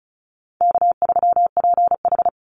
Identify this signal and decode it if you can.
Morse code, heard as a keyed tone, about 34 wpm, 700 Hz. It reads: K3PH
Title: Easy Call Signs Repeated 2x - 35wpm